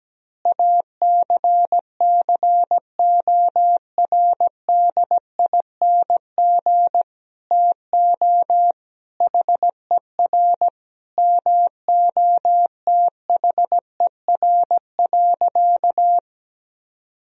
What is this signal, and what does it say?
Morse code, heard as a keyed tone, about 17 wpm, 700 Hz. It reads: ACCORDING TO HER MOTHER.
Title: Sentences from Top 1000 Words - Review Part 4 - 17wpm